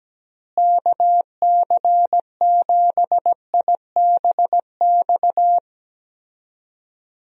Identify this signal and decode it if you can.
Morse code, heard as a keyed tone, about 17 wpm, 700 Hz. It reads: KC7IBX
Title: USA Call Signs - 6 Characters - Repeated 3x - 17wpm